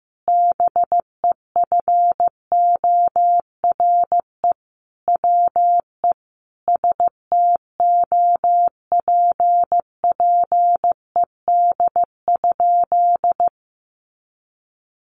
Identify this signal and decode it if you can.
Morse code, heard as a keyed tone, about 15 wpm, 700 Hz. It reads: BEFORE WE STOPPED?
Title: A Sound Of Thunder 15wpm